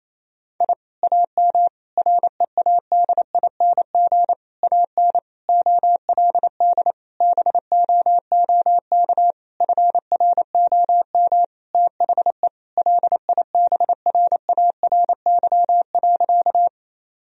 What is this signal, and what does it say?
Morse code, heard as a keyed tone, about 28 wpm, 700 Hz. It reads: I AM READING AN OLD BOOK FROM THE LIBRARY.
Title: Sentences from Top 1000 Words - Review Part 4 - 28wpm